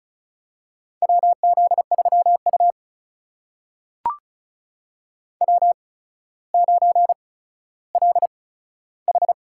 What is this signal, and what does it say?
Morse code, heard as a keyed tone, about 36 wpm, spaced as for 15 wpm, 700 Hz. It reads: WZ3U W9LH
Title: USA Call Signs - 4 Characters - 8x Character Spacing - 35wpm